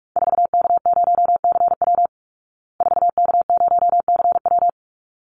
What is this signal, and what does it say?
Morse code, heard as a keyed tone, about 44 wpm, 700 Hz. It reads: 4K0CW 4K0CW
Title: Call Signs Repeated 3x - 45wpm